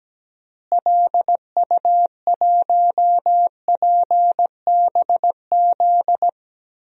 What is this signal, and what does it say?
Morse code, heard as a keyed tone, about 17 wpm, 700 Hz. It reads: LU1PBZ
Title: DXCC-100 Call Signs - 6 Characters - Repeated 2x - 17wpm